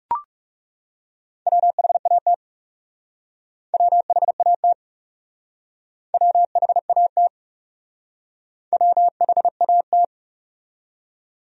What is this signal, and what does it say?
Morse code, heard as a keyed tone, about 40 wpm, 700 Hz. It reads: WHAT WHAT WHAT WHAT